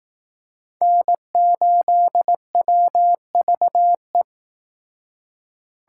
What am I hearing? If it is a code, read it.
Morse code, heard as a keyed tone, about 18 wpm, 700 Hz. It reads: N8WVE
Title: Speed Racing - USA 1x3 Call Signs 15wpm